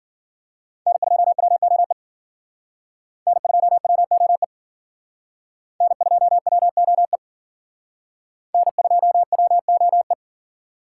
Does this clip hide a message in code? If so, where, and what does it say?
Yes: Morse code from 0.9 to 10.1 s, heard as a keyed tone, about 52 wpm, 700 Hz. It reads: N2WOE N2WOE N2WOE N2WOE